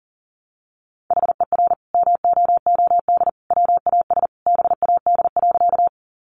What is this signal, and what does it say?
Morse code, heard as a keyed tone, about 38 wpm, 700 Hz. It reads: HER MOOD WAS BAD.
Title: Sentences from Top 1100 Words 40wpm